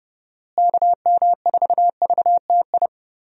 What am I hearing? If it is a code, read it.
Morse code, heard as a keyed tone, about 30 wpm, 700 Hz. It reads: KM4VTI
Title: USA Call Signs - 6 Characters - Repeated 2x - 30wpm